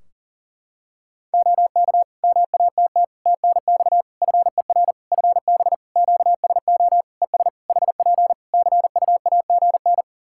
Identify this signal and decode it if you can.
Morse code, heard as a keyed tone, about 38 wpm, 700 Hz. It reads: OK MATT TNX FER FB QSO ES HP CUAGN
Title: Practice QSO - Exchange 4 - Ending - 40wpm